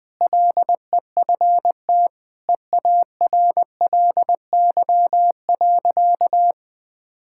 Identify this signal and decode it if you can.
Morse code, heard as a keyed tone, about 21 wpm, 700 Hz. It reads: LEFT EARLY.
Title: Sentences with 3-Word Phrases - 20wpm